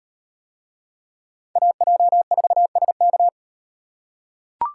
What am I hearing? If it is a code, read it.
Morse code, heard as a keyed tone, about 38 wpm, 700 Hz. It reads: AJ4SK